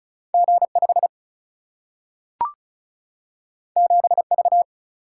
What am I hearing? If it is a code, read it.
Morse code, heard as a keyed tone, about 36 wpm, 700 Hz. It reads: G 5 7 V
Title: Single-Letter Single-Number 35wpm